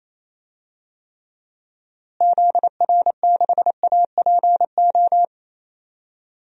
Morse code, heard as a keyed tone, about 28 wpm, 700 Hz. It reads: ZR6APO
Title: DXCC-100 Call Signs - 6 Characters - Repeated 3x - 28wpm